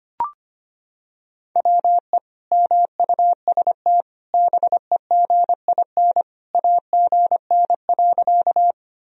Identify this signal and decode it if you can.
Morse code, heard as a keyed tone, about 25 wpm, 700 Hz. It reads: WE MUST BEGIN AGN.
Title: Sentences from Top 300 Words and Abbreviations 25wpm